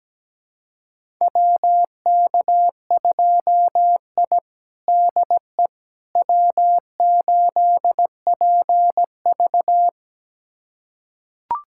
Morse code, heard as a keyed tone, about 17 wpm, 700 Hz. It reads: WK2I DE W8PV
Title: USA Callsign DE Callsign - 4 Characters - 17wpm